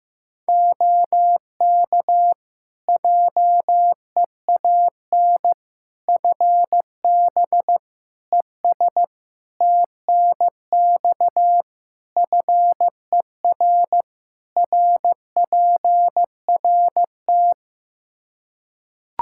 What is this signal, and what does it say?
Morse code, heard as a keyed tone, about 15 wpm, 700 Hz. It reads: OK JEAN FB ES TNX FER RPRT
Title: Practice QSO - Exchange 2 - Info - 15wpm